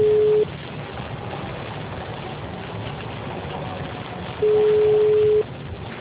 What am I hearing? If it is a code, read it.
Ringback tone (Europe, 425 Hz).
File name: out- -2014- - - .10243